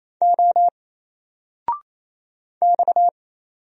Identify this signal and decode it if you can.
Morse code, heard as a keyed tone, about 29 wpm, 700 Hz. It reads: O X
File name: Single-Letter-Number-28wpm